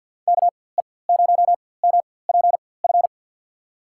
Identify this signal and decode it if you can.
Morse code, heard as a keyed tone, about 48 wpm, spaced as for 23 wpm, 700 Hz. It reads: KE0MPF
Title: USA Call Signs - 6 Characters - 4x Character Spacing - 50wpm